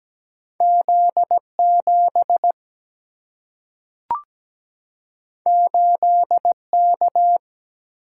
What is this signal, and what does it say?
Morse code, heard as a keyed tone, about 17 wpm, 700 Hz. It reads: Z7 8K
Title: Single-Letter Single-Number 17wpm